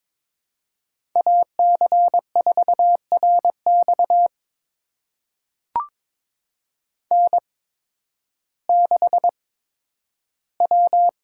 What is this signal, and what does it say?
Morse code, heard as a keyed tone, about 22 wpm, spaced as for 12 wpm, 700 Hz. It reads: AC4RX N6W
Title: USA Call Signs - 5 Characters - 8x Character Spacing - 22wpm